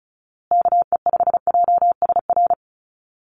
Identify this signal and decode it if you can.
Morse code, heard as a keyed tone, about 34 wpm, 700 Hz. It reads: KE5JSR
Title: USA Call Signs Repeated 3x 35wpm